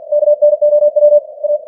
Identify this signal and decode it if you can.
Morse code, heard as a keyed tone, about 50 wpm, 590 Hz. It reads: KNOW A
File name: archived / CWBros